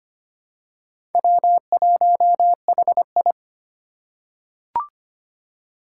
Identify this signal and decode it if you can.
Morse code, heard as a keyed tone, about 25 wpm, 700 Hz. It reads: W1HI
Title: Four Challenging Letter-Number Combinations 25wpm